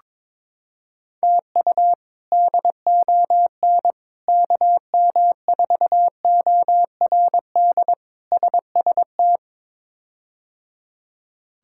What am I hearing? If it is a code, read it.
Morse code, heard as a keyed tone, about 22 wpm, 700 Hz. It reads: TU DON KM4ORD SST
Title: K1USN SST QSO Practice - 22wpm